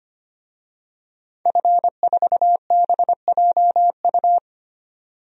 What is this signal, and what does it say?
Morse code, heard as a keyed tone, about 25 wpm, 700 Hz. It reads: F4BJU
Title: DXCC-100 Call Signs - 5 Characters - 25wpm